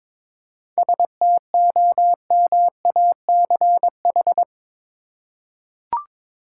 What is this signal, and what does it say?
Morse code, heard as a keyed tone, about 22 wpm, 700 Hz. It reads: STOMACH
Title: Seven Letter Words - 22wpm